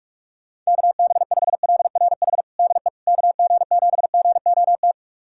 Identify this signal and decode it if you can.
Morse code, heard as a keyed tone, about 46 wpm, 700 Hz. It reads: KB5LRH DE KG7GOT
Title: USA Callsign DE Callsign 45wpm